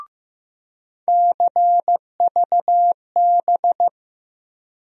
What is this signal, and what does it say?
Morse code, heard as a keyed tone, about 15 wpm, 700 Hz. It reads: CVB
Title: USA Call Sign Postfixes - 3 Letters 15wpm